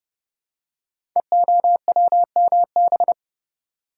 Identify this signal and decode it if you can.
Morse code, heard as a keyed tone, about 31 wpm, 700 Hz. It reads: EOWMB
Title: Code Groups - 5 Letter:Numbers - 30wpm